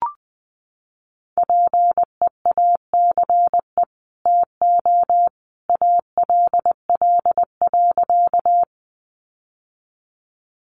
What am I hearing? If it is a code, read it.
Morse code, heard as a keyed tone, about 19 wpm, 700 Hz. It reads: PEACE TO ALL.
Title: Sentences from Top 1500 Words 20wpm